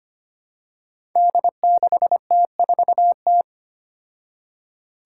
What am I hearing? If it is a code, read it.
Morse code, heard as a keyed tone, about 25 wpm, 700 Hz. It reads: D6T4T
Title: Five Challenging Letter-Number Combinations 25wpm